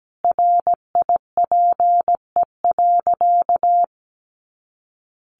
Morse code, heard as a keyed tone, about 17 wpm, 700 Hz. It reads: RIPE.
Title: Additional Easy Phrases 17wpm